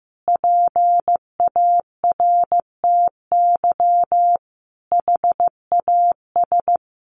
Morse code, heard as a keyed tone, about 15 wpm, 700 Hz. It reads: PARTY HAS
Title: Sentences from Top 1500 Words 15wpm